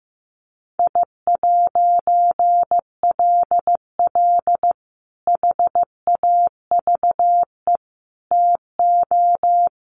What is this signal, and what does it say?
Morse code, heard as a keyed tone, about 15 wpm, 700 Hz. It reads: I'LL HAVE TO
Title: A Sound Of Thunder 15wpm